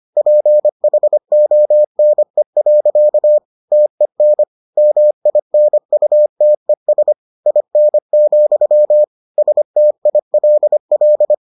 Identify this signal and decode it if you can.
Morse code, heard as a keyed tone, about 25 wpm, 600 Hz. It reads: PHONE. TEN MINUTES IN, STILL